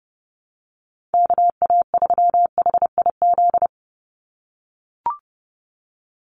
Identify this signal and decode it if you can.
Morse code, heard as a keyed tone, about 30 wpm, 700 Hz. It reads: KA3HIZ
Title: USA Call Signs 30wpm